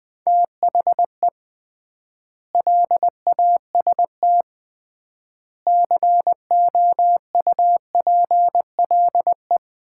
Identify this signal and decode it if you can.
Morse code, heard as a keyed tone, about 20 wpm, 700 Hz. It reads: THE LAST COUPLE
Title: 3x Word-Spacing - Sets of 5 Words - 20wpm